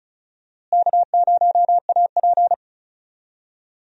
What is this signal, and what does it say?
Morse code, heard as a keyed tone, about 36 wpm, 700 Hz. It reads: K0AP
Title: USA Call Signs - 4 Characters - Repeated 2x - 35wpm